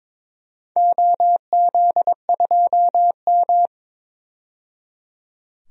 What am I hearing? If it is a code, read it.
Morse code, heard as a keyed tone, about 22 wpm, 700 Hz. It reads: OZ2M
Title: DXCC-100 Call Signs - 4 Characters - 22wpm